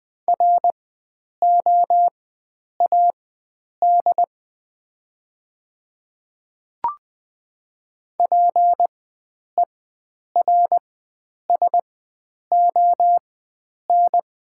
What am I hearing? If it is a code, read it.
Morse code, heard as a keyed tone, about 20 wpm, spaced as for 9 wpm, 700 Hz. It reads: ROAD PERSON
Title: 4x Character Spacing - Top 500 Words - 20wpm